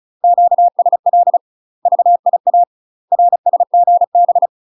Morse code, heard as a keyed tone, about 34 wpm, 700 Hz. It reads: QSL VIA RSGB
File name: UK MASTER QSOs TRAINING / 35WPM / UK MASTER QSOs 0039